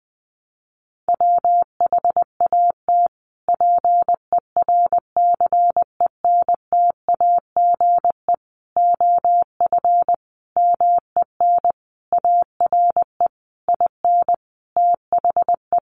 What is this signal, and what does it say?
Morse code, heard as a keyed tone, about 20 wpm, 700 Hz. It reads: WHAT PERCENTAGE OF MEN ARE IN THE
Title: Sentences from Top 1300 Words 20wpm